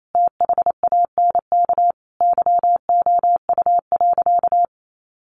Morse code, heard as a keyed tone, about 28 wpm, 700 Hz. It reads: THANK YOU.
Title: Sentences from Top 1100 Words 28wpm